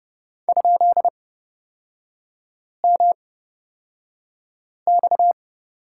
Morse code, heard as a keyed tone, about 29 wpm, spaced as for 5 wpm, 700 Hz. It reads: ?MX
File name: Lesson 38e - Letter q - Three Characters 30wpm